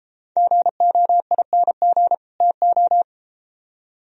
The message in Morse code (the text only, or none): GOING TO